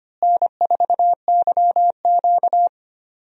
Morse code, heard as a keyed tone, about 25 wpm, 700 Hz. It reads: N4YQ
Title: USA Call Signs - 4 Characters - Repeated 2x - 25wpm